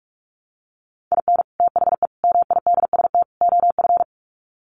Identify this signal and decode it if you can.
Morse code, heard as a keyed tone, about 44 wpm, 700 Hz. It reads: IN THE MIDST OF
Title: Sets of 4 Words 45wpm